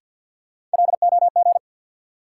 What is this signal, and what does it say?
Morse code, heard as a keyed tone, about 46 wpm, 700 Hz. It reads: LOG
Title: Sentences from Top 1400 Words - New Word Drill - 50wpm